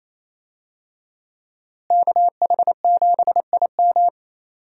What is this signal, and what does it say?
Morse code, heard as a keyed tone, about 28 wpm, 700 Hz. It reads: KH7IM
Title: USA Call Signs - 5 Characters - Repeated 3x - 28wpm